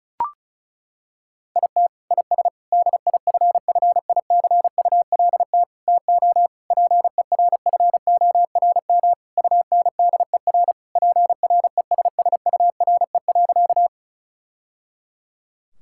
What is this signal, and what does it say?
Morse code, heard as a keyed tone, about 34 wpm, 700 Hz. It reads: IT IS DIFFICULT TO PERFORM UNDER PRESSURE.
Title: Sentences from Top 1000 Words - Review Part 6 - 35wpm